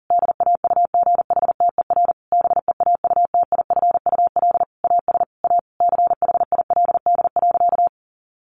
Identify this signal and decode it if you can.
Morse code, heard as a keyed tone, about 40 wpm, 700 Hz. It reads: DAUGHTER BEAUTIFUL AS A CHILD.
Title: Sentences from Top 1500 Words 40wpm